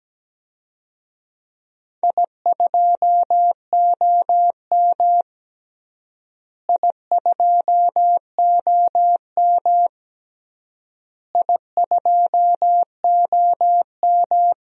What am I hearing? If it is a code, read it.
Morse code, heard as a keyed tone, about 17 wpm, 700 Hz. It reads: I2OM I2OM I2OM
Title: DXCC-100 Call Signs - 4 Characters - Repeated 3x - 17wpm